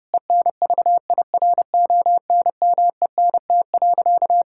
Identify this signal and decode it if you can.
Morse code, heard as a keyed tone, about 32 wpm, 700 Hz. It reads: ENVIRONMENT.
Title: Sentences from Top 1000 Words - Review Part 6 - 30wpm